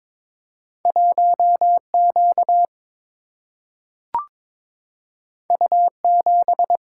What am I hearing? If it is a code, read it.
Morse code, heard as a keyed tone, about 23 wpm, 700 Hz. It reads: 1Q U7
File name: practice / 22wpm / Two Letter-Number Combinations - 22wpm